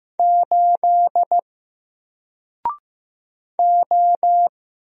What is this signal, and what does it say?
Morse code, heard as a keyed tone, about 15 wpm, 700 Hz. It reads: 8 O
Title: Single Letter-Number 15wpm